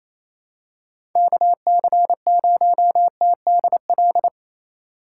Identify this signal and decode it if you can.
Morse code, heard as a keyed tone, about 28 wpm, 700 Hz. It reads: KC0TDL
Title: USA Call Signs - 6 Characters - 3x Character Spacing - 28wpm